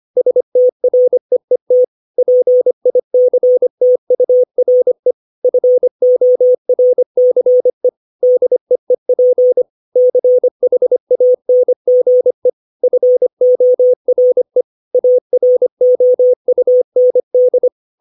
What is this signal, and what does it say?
Morse code, heard as a keyed tone, about 25 wpm, 500 Hz. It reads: STREET PICTURE FORCE DEEP CHANGE FORE AROUND
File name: 632 words / 632 Words 25WPM